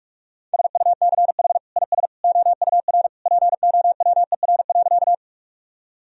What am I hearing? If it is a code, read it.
Morse code, heard as a keyed tone, about 46 wpm, 700 Hz. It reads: SUCH IS OUR POWER.
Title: Sentences from Top 1000 Words - Review Part 2 - 45wpm